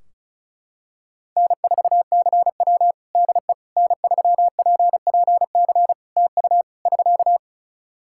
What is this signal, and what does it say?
Morse code, heard as a keyed tone, about 34 wpm, 700 Hz. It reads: N4CW DE N3PPC TU <SK>
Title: Practice QSO - Exchange 4 - Ending - 35wpm